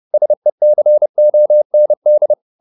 Morse code, heard as a keyed tone, about 30 wpm, 600 Hz. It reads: SECOND
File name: the old man and the sea / the old man and the sea 1 30wpm